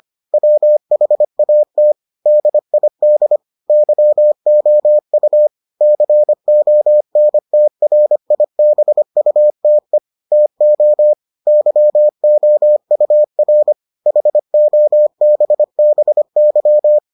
Morse code, heard as a keyed tone, about 25 wpm, 600 Hz. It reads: WHAT DID YOU CONTRIBUTE TO YOUR HOBBY